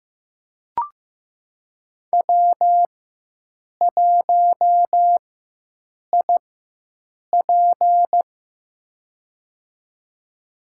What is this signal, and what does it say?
Morse code, heard as a keyed tone, about 15 wpm, spaced as for 7 wpm, 700 Hz. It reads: W1IP
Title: USA Call Signs - 4 Characters - 4x Character Spacing - 15wpm